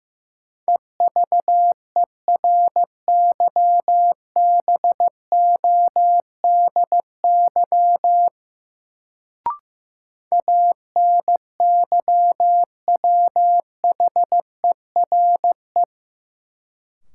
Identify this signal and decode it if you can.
Morse code, heard as a keyed tone, about 15 wpm, 700 Hz. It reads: EVERYBODY ANYWHERE
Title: Sentences from Top 1400 Words - New Word Drill - 15wpm